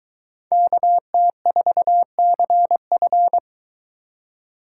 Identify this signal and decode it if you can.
Morse code, heard as a keyed tone, about 23 wpm, 700 Hz. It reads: KT4CF